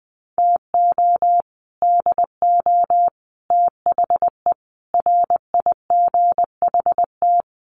Morse code, heard as a keyed tone, about 20 wpm, 700 Hz. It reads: TO DO THE RIGHT
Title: Sets of 5 Words 20wpm